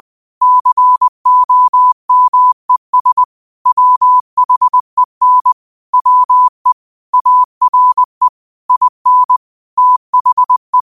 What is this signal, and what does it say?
Morse code, heard as a keyed tone, about 20 wpm, 1000 Hz. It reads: COMES WHEN WE ARE IN THE